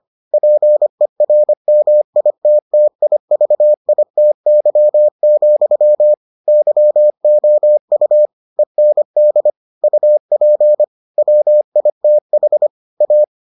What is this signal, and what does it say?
Morse code, heard as a keyed tone, about 25 wpm, 600 Hz. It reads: PERMITTIVITY, YOU END UP WITH A